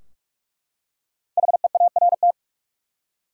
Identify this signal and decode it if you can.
Morse code, heard as a keyed tone, about 43 wpm, 700 Hz. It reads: HEART
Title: Five Letter Words - 45wpm